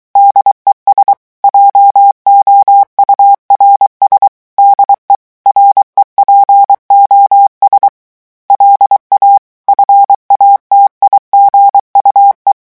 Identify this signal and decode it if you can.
Morse code, heard as a keyed tone, about 23 wpm, 800 Hz. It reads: DES JOURS DE REPOS LA FATIGUE
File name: morse mp3 / revision9 citations humoristiques vitesse 22 mots minute